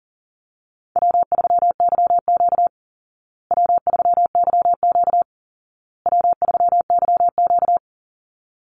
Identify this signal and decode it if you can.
Morse code, heard as a keyed tone, about 40 wpm, 700 Hz. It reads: W3YQ W3YQ W3YQ
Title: Call Signs Repeated 2x - 40wpm